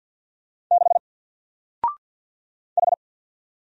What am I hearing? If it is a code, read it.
Morse code, heard as a keyed tone, about 50 wpm, 700 Hz. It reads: 6 H
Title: Single Character (Letter:Number:Punctuation:Prosign) 50wpm